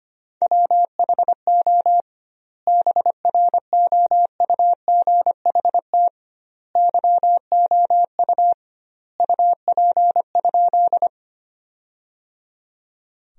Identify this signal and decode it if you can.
Morse code, heard as a keyed tone, about 25 wpm, 700 Hz. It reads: WHO BROUGHT YOU UP?
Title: 2x Word-Spacing - Sentences from Top 500 Words - 25wpm